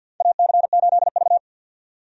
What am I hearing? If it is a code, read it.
Morse code, heard as a keyed tone, about 52 wpm, 700 Hz. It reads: AC8V
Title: USA Call Signs - 4 Characters - 50wpm